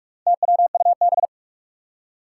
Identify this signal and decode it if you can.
Morse code, heard as a keyed tone, about 44 wpm, 700 Hz. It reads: TWUB
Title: Four Challenging Letter-Number Combinations 45wpm